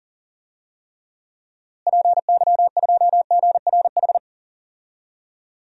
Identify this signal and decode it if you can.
Morse code, heard as a keyed tone, about 40 wpm, 700 Hz. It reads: PY2GRH
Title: DXCC-100 Call Signs - 6 Characters - Repeated 2x - 40wpm